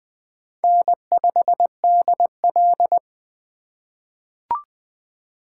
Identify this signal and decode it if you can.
Morse code, heard as a keyed tone, about 20 wpm, 700 Hz. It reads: N5DL
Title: USA Call Signs - 4 Characters - 2x Character Spacing - 20wpm